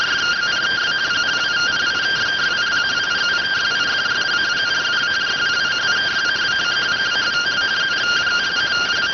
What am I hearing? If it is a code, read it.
Two-tone teleprinter signal, about 170 Hz shift.